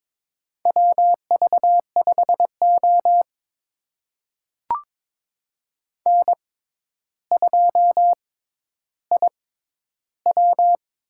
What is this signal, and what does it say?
Morse code, heard as a keyed tone, about 22 wpm, spaced as for 11 wpm, 700 Hz. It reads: WV5O N2IW
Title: USA Call Signs - 4 Characters - 6x Character Spacing - 22wpm